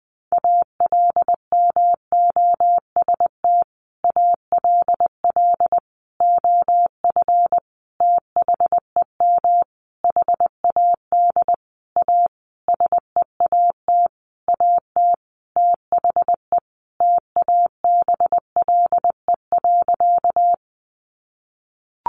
Morse code, heard as a keyed tone, about 20 wpm, 700 Hz. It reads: ALMOST ALL OF THEM HAD A SEAT AT THE TABLE.
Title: Sentences from Top 1500 Words 20wpm